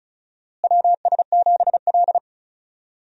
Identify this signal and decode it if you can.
Morse code, heard as a keyed tone, about 35 wpm, 700 Hz. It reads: WS7L
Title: USA Call Signs - 4 Characters - Repeated 2x - 35wpm